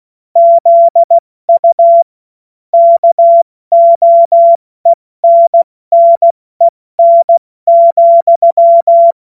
Morse code, heard as a keyed tone, about 16 wpm, 680 Hz. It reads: ZU KOENNEN,
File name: Morsekurs / Witze / Witz 25. Oktober 2022 70 F80